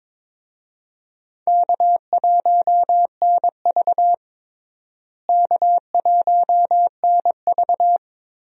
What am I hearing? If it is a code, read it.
Morse code, heard as a keyed tone, about 22 wpm, 700 Hz. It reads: K1NV K1NV
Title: USA Call Signs - 4 Characters - Repeated 3x - 22wpm